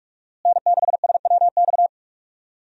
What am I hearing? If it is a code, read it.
Morse code, heard as a keyed tone, about 46 wpm, 700 Hz. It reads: N6SWX
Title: USA Call Signs - 5 Characters - Repeated 2x - 45wpm